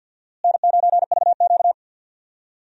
Morse code, heard as a keyed tone, about 50 wpm, 700 Hz. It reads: N9VQ
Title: USA Call Signs - 4 Characters - Repeated 3x - 50wpm